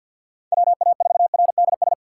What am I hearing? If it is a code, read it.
Morse code, heard as a keyed tone, about 49 wpm, 700 Hz. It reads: WA4RDS
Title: USA Call Signs - 6 Characters - 3x Character Spacing - 50wpm